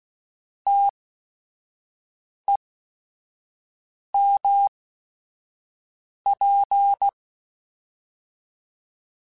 Morse code, heard as a keyed tone, about 16 wpm, spaced as for 5 wpm, 800 Hz. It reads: TEMP